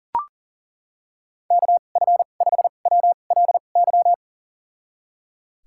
Morse code, heard as a keyed tone, about 37 wpm, spaced as for 28 wpm, 700 Hz. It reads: KF5WLY
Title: USA Call Signs - 6 Characters - 2x Character Spacing - 40wpm